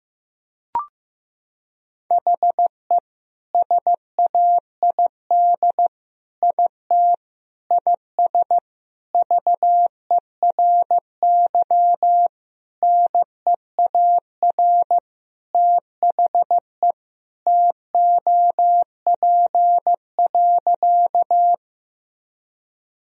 Morse code, heard as a keyed tone, about 15 wpm, 700 Hz. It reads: HE SAID IT IS VERY NEAR THE TOP.
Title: Sentences from Top 1000 Words - Review Part 2 - 15wpm